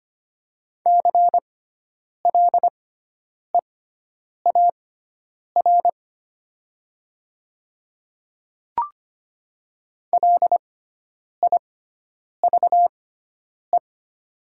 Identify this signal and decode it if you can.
Morse code, heard as a keyed tone, about 25 wpm, spaced as for 9 wpm, 690 Hz. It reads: CLEAR LIVE
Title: 6x Character Spacing - Top 500 Words - 25wpm